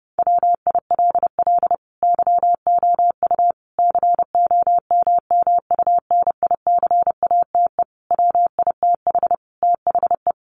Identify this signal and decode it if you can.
Morse code, heard as a keyed tone, about 30 wpm, 700 Hz. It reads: WILL YOU COMMUNICATE WITH THE
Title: Sentences from Top 1300 Words 30wpm